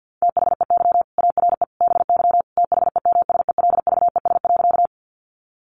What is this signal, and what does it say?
Morse code, heard as a keyed tone, about 50 wpm, 700 Hz. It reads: THEY ARE BY THEMSELVES.
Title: Sentences from Top 800 Words 50wpm